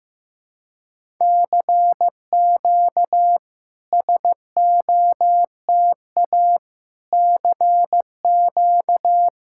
Morse code, heard as a keyed tone, about 15 wpm, 690 Hz. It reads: CQ SOTA CQ
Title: SOTA QSO Practice 15wpm